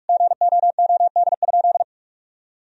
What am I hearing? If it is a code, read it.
Morse code, heard as a keyed tone, about 41 wpm, 700 Hz. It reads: GOOD?
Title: Sentences from Top 1000 Words - Review Part 3 - 45wpm